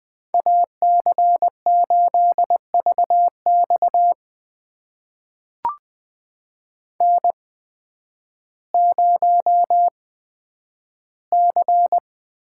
Morse code, heard as a keyed tone, about 20 wpm, spaced as for 11 wpm, 700 Hz. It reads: AC8VX N0C
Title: USA Call Signs - 5 Characters - 8x Character Spacing - 20wpm